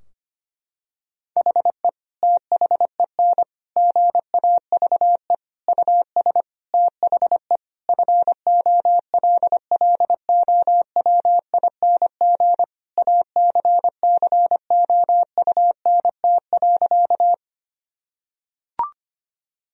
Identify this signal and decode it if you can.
Morse code, heard as a keyed tone, about 25 wpm, 700 Hz. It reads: HE THEN GAVE US THE FOLLOWING ACCOUNT.
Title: Sentences from Top 1000 Words - Review Part 4 - 25wpm